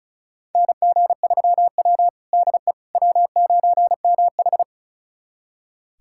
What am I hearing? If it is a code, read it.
Morse code, heard as a keyed tone, about 36 wpm, 700 Hz. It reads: NG3W DE W9MH
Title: USA Callsign DE Callsign - 4 Characters - 35wpm